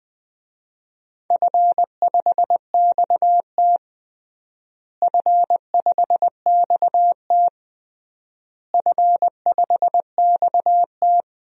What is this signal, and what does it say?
Morse code, heard as a keyed tone, about 20 wpm, 700 Hz. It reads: F5XT F5XT F5XT
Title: DXCC-100 Call Signs - 4 Characters - Repeated 3x - 20wpm